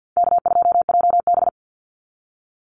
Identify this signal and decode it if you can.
Morse code, heard as a keyed tone, about 50 wpm, 700 Hz. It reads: K2JB
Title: NA SOTA Chaser Callsigns 50wpm